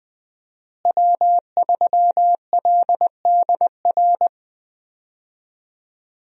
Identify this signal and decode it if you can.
Morse code, heard as a keyed tone, about 21 wpm, 700 Hz. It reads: W3LDR
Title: USA Call Signs - 5 Characters - Repeated 3x - 20wpm